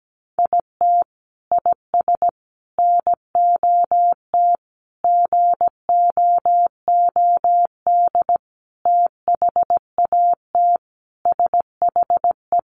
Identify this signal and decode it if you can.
Morse code, heard as a keyed tone, about 17 wpm, 700 Hz. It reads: IT IS NOT GOOD THAT SHE
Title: Sentences from Top 600 Words 17wpm